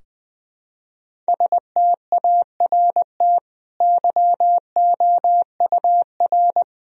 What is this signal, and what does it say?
Morse code, heard as a keyed tone, about 20 wpm, 700 Hz. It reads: START YOUR
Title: Sentences from Top 1000 Words - Review Part 2 - 20wpm